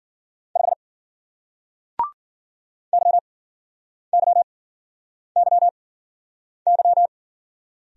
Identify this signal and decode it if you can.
Morse code, heard as a keyed tone, about 50 wpm, spaced as for 9 wpm, 700 Hz. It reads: 5 YYYY